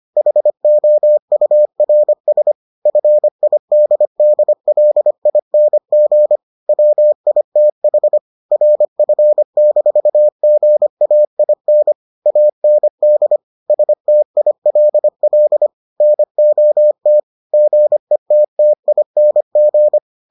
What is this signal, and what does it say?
Morse code, heard as a keyed tone, about 25 wpm, 600 Hz. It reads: HOURS FIDDLING WITH RF-GAIN AND STILL NOT GETTING